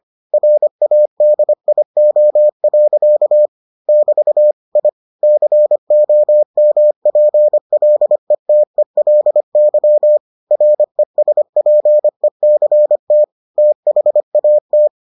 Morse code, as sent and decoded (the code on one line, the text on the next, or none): .-. .- -.. .. --- .-.-.- / -...- / .. / -.-. --- -- .--. .-.. - . .-.. -.-- / .-. ... .--. -.-. - / - .... .- -
RADIO. <BT> I COMPLETELY RESPECT THAT